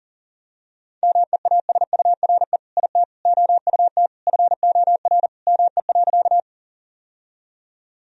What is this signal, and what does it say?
Morse code, heard as a keyed tone, about 40 wpm, 700 Hz. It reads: MEASURE IT OUT FOR ME.
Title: Sentences from Top 1000 Words - Review Part 2 - 40wpm